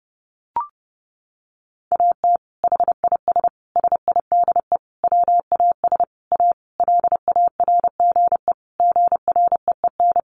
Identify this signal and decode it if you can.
Morse code, heard as a keyed tone, about 30 wpm, 700 Hz. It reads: AT HIS SIDE WAS A LARGE GREEN
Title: Sentences from Top 500 Words 30wpm